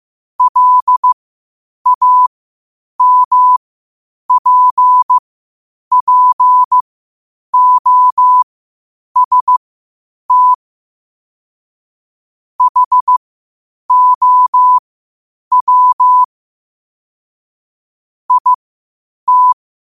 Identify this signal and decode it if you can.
Morse code, heard as a keyed tone, about 15 wpm, spaced as for 8 wpm, 1000 Hz. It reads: LAMPPOST HOW IT
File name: qotdfiles / 8wpm